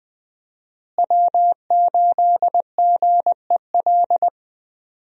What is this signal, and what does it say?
Morse code, heard as a keyed tone, about 20 wpm, 700 Hz. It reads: W8GEL
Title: USA Call Signs - 5 Characters - Repeated 2x - 20wpm